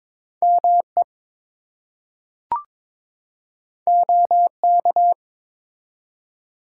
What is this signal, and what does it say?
Morse code, heard as a keyed tone, about 22 wpm, 700 Hz. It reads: ME OK
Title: Two Letter Words - 22wpm